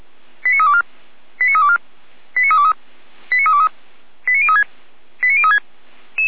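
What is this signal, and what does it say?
Five-tone selective call (ZVEI), heard as quick runs of five tones, tones of about 70 ms. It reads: 89324 89324 89323 89323 89036 89036
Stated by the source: ZVEI five-tone call 89283 received by SDR.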